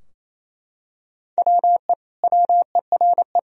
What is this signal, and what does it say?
Morse code, heard as a keyed tone, about 28 wpm, 700 Hz. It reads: WE WERE
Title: Sentences from Top 1000 Words - Review Part 4 - 28wpm